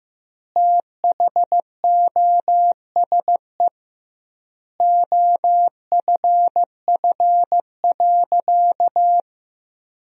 Morse code, heard as a keyed tone, about 15 wpm, 700 Hz. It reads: THOSE OFF.
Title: 2x Word-Spacing - Sentences from Top 500 Words - 15wpm